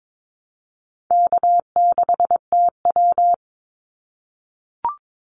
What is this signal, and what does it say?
Morse code, heard as a keyed tone, about 22 wpm, 700 Hz. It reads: K6TW
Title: NA SOTA Activator Callsigns 22wpm